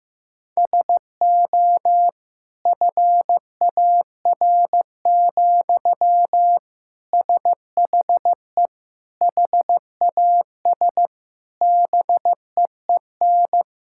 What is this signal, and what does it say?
Morse code, heard as a keyed tone, about 15 wpm, 700 Hz. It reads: SO FAR, SHE HAS BEEN
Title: Sentences from Top 1000 Words - Review Part 6 - 15wpm